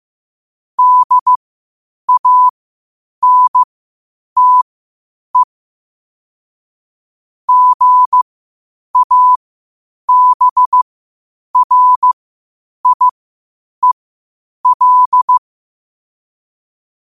Morse code, heard as a keyed tone, about 15 wpm, spaced as for 8 wpm, 1000 Hz. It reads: DANTE GABRIEL
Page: Quotes for Fri, 15 Aug 2025 in Morse Code at 8 words per minute.